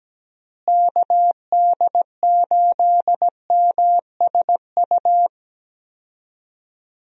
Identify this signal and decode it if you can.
Morse code, heard as a keyed tone, about 17 wpm, 700 Hz. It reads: KD8MSU
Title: USA Call Signs - 6 Characters - Repeated 3x - 17wpm